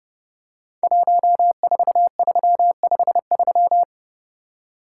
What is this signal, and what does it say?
Morse code, heard as a keyed tone, about 30 wpm, 700 Hz. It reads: 14353
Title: Code Groups - 5 Letter:Numbers - 30wpm